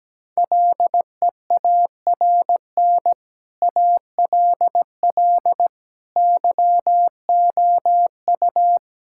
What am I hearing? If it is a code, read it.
Morse code, heard as a keyed tone, about 17 wpm, 700 Hz. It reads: LEARN ALL YOU
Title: Sentences from Top 1000 Words - Review Part 5 - 17wpm